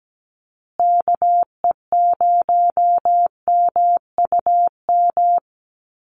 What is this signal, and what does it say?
Morse code, heard as a keyed tone, about 17 wpm, 700 Hz. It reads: KE0MUM
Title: USA Call Signs Repeated 3x 17wpm